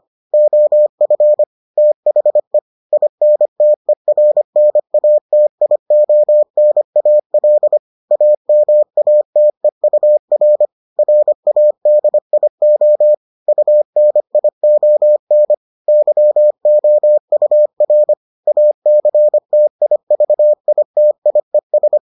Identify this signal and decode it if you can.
Morse code, heard as a keyed tone, about 25 wpm, 600 Hz. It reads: OF THE INTERNATIONAL AMATEUR RADIO UNION YOUR ACTIVITIES